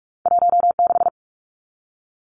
Morse code, heard as a keyed tone, about 46 wpm, 700 Hz. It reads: 16
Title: Numbers - 2-Digit 45wpm